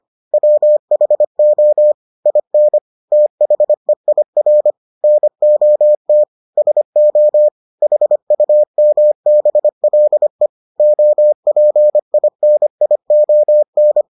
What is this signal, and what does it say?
Morse code, heard as a keyed tone, about 25 wpm, 600 Hz. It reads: WHO IN THEIR NOT SO HUMBLE OPINION